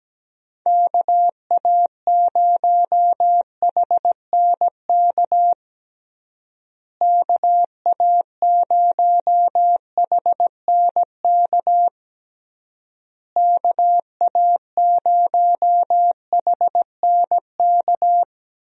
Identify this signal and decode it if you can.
Morse code, heard as a keyed tone, about 17 wpm, 700 Hz. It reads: KA0HNK KA0HNK KA0HNK
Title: USA Call Signs - 6 Characters - Repeated 3x - 17wpm